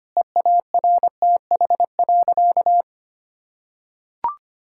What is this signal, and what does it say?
Morse code, heard as a keyed tone, about 25 wpm, 700 Hz. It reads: EARTH.
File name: practice / 25wpm / Difficult Sentences with Hints - 25wpm